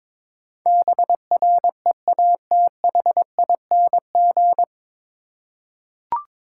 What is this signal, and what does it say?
Morse code, heard as a keyed tone, about 22 wpm, 700 Hz. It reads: BREATHING
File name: 9 Letter Words - First 3 Letters Given - 22wpm